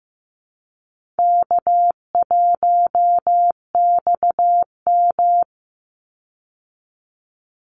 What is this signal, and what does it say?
Morse code, heard as a keyed tone, about 15 wpm, 700 Hz. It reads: K1XM
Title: Call Signs Repeated 2x - 15wpm